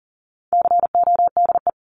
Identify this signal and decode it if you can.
Morse code, heard as a keyed tone, about 40 wpm, 700 Hz. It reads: CODE
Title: Top 1501-1750 Words 40wpm